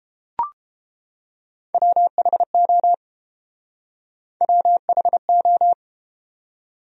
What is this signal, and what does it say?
Morse code, heard as a keyed tone, about 29 wpm, 700 Hz. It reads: WHO WHO